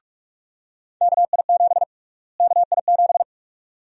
Morse code, heard as a keyed tone, about 48 wpm, 700 Hz. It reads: KI7 KI7
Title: USA Call Sign Prefixes - Repeated 2x - 45wpm